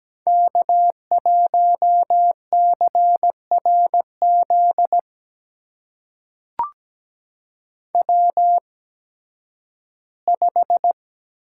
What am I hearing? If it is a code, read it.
Morse code, heard as a keyed tone, about 17 wpm, spaced as for 11 wpm, 700 Hz. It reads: K1CRZ W5
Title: USA Call Signs - 5 Characters - 8x Character Spacing - 17wpm